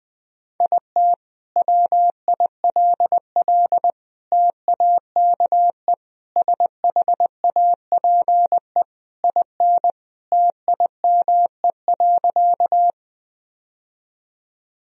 Morse code, heard as a keyed tone, about 20 wpm, 700 Hz. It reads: IT WILL TAKE SHAPE IN TIME.
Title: Sentences from Top 1000 Words - Review Part 3 - 20wpm